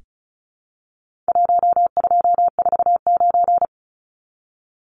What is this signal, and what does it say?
Morse code, heard as a keyed tone, about 34 wpm, 700 Hz. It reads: 1249
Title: Numbers - 4-Digit 35wpm